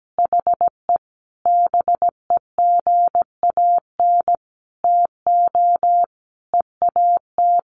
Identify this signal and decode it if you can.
Morse code, heard as a keyed tone, about 17 wpm, 700 Hz. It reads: HE BEGAN TO EAT
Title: Sentences from Top 1100 Words 17wpm